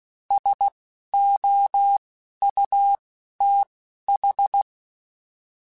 Morse code, heard as a keyed tone, about 16 wpm, spaced as for 11 wpm, 800 Hz. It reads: SOUTH
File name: (spaced) / QSOs Spaced-mp3s / kc7haa @15 sp